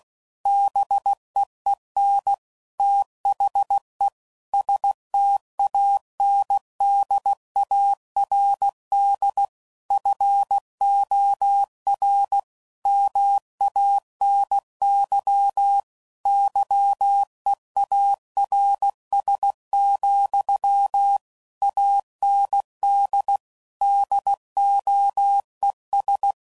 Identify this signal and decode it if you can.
Morse code, heard as a keyed tone, about 16 wpm, 800 Hz. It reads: BEEN THE STANDARD FOR MANY YEARS, AND DOES
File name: CW Code Groups @15WPM / Technical Subjects / AND Gate